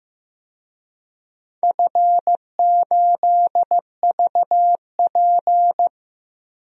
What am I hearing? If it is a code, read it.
Morse code, heard as a keyed tone, about 16 wpm, 700 Hz. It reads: F8VP
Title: DXCC-100 Call Signs - 4 Characters - Repeated 3x - 15wpm